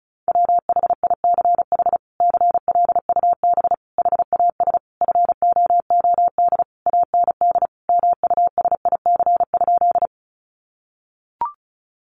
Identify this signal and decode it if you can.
Morse code, heard as a keyed tone, about 34 wpm, 700 Hz. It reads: WHICH CLUB HAS FOOD AND MUSIC?
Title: Sentences from Top 1200 Words 35wpm